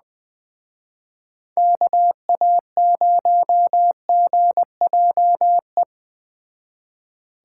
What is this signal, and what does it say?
Morse code, heard as a keyed tone, about 21 wpm, 700 Hz. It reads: KA0GJE